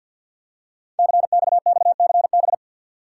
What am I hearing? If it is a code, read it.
Morse code, heard as a keyed tone, about 50 wpm, 690 Hz. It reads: CXXCB
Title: Code Groups - 5 Letters - 50wpm